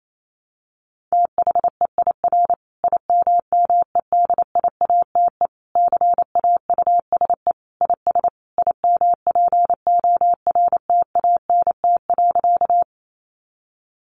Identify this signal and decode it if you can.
Morse code, heard as a keyed tone, about 28 wpm, 700 Hz. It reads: THEIR IMMEDIATE CAUSE IS IMPORTANT.
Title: Sentences from Top 1200 Words 28wpm